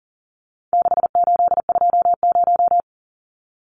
Morse code, heard as a keyed tone, about 37 wpm, 700 Hz. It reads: 6820
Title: Numbers - 4-Digit 40wpm